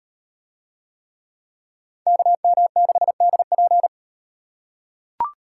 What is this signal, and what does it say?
Morse code, heard as a keyed tone, about 38 wpm, 700 Hz. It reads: KM6DP